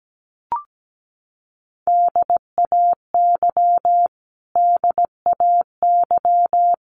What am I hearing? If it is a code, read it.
Morse code, heard as a keyed tone, about 17 wpm, 700 Hz. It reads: DAY DAY
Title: Top 500 Words Repeated 17wpm